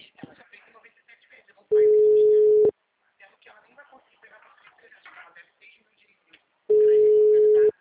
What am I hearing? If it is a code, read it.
Ringback tone (Europe, 425 Hz), with speech.